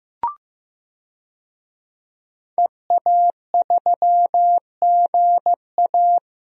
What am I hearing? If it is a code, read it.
Morse code, heard as a keyed tone, about 16 wpm, 700 Hz. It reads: EA3GA
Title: DXCC-100 Call Signs - 5 Characters - Repeated 3x - 15wpm